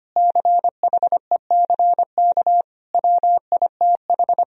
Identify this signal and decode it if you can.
Morse code, heard as a keyed tone, about 25 wpm, 700 Hz. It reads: CHECK WITH
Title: Sentences from Top 1000 Words - Review Part 3 - 25wpm